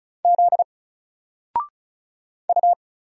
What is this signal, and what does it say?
Morse code, heard as a keyed tone, about 36 wpm, 700 Hz. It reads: Z U
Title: Single Character (Letter:Number:Punctuation:Prosign) 35wpm